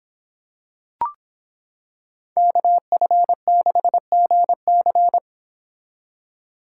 Morse code, heard as a keyed tone, about 25 wpm, 700 Hz. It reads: KF6GC